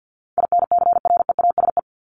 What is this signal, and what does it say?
Morse code, heard as a keyed tone, about 50 wpm, 700 Hz. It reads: INCREASE
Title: LY Suffix - Top 100 Words with Base Words 50wpm